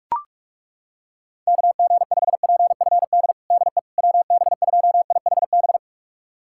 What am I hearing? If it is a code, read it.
Morse code, heard as a keyed tone, about 42 wpm, 700 Hz. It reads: KG5PFD DE WB2IHB
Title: USA Callsign DE Callsign 45wpm